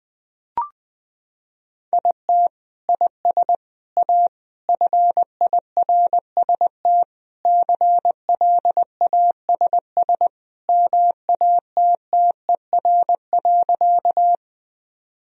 Morse code, heard as a keyed tone, about 20 wpm, 700 Hz. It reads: IT IS A FIRST CLASS MATTER.
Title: Sentences from Top 1000 Words - Review Part 2 - 20wpm